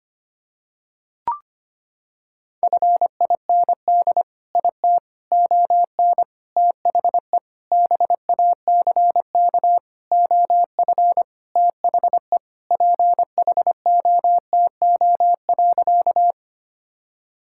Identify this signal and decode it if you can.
Morse code, heard as a keyed tone, about 25 wpm, 700 Hz. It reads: FIND IT ON THE BACK OF THE PHOTO.
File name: practice / 25wpm / Sentences with 4-Word Phrases - 25wpm